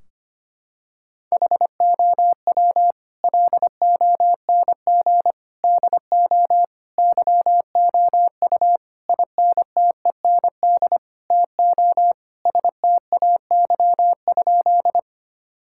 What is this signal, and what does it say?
Morse code, heard as a keyed tone, about 25 wpm, 700 Hz. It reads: HOW LONG DO YOU INTEND TO STAY?
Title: Sentences from Top 1600 Words 25wpm